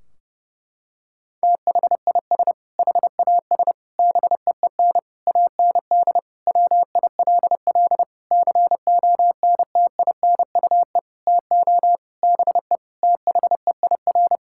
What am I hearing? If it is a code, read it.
Morse code, heard as a keyed tone, about 30 wpm, 700 Hz. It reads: THIS HAS BEEN AND WILL CONTINUE TO BE THEIR
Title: Sentences from Top 1600 Words 30wpm